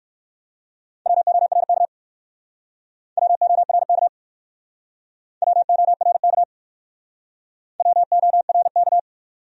Morse code, heard as a keyed tone, about 58 wpm, 700 Hz. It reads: WORK WORK WORK WORK